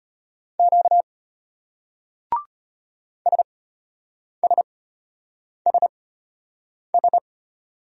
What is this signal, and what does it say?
Morse code, heard as a keyed tone, about 33 wpm, spaced as for 8 wpm, 700 Hz. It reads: Q SSSS